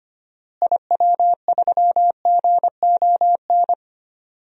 Morse code, heard as a keyed tone, about 25 wpm, 700 Hz. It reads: IW3GON
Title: DXCC-100 Call Signs - 6 Characters - Repeated 3x - 25wpm